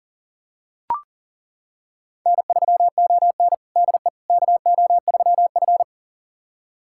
Morse code, heard as a keyed tone, about 40 wpm, 700 Hz. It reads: N3ON DE KO3F
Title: USA Callsign DE Callsign - 4 Characters - 40wpm